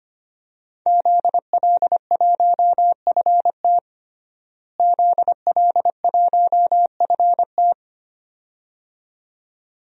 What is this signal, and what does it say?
Morse code, heard as a keyed tone, about 25 wpm, 700 Hz. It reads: ZL1FT ZL1FT
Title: DXCC-100 Call Signs - 5 Characters - Repeated 2x - 25wpm